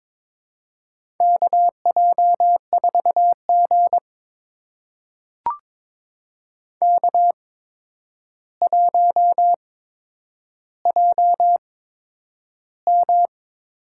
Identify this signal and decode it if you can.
Morse code, heard as a keyed tone, about 22 wpm, spaced as for 9 wpm, 700 Hz. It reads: KJ4G K1JM
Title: USA Call Signs - 4 Characters - 8x Character Spacing - 22wpm